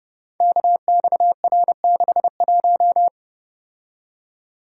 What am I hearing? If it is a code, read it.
Morse code, heard as a keyed tone, about 30 wpm, 700 Hz. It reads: KXR61
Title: Code Groups - 5 Letter:Numbers - 30wpm